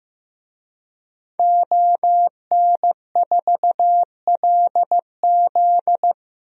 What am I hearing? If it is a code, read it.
Morse code, heard as a keyed tone, about 15 wpm, 700 Hz. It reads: ON4LZ
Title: DXCC-100 Call Signs - 5 Characters - Repeated 3x - 15wpm